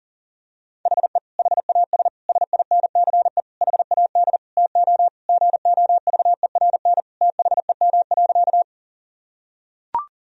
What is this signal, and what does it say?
Morse code, heard as a keyed tone, about 40 wpm, 700 Hz. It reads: HE HAS SINCE HAD TO GOVERN THEM.
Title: Sentences from Top 1000 Words - Review Part 2 - 40wpm